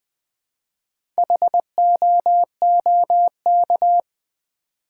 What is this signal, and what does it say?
Morse code, heard as a keyed tone, about 20 wpm, 700 Hz. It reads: HOOK
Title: Build A Word - 1 Letter At A Time - 4-Letter Words - 20wpm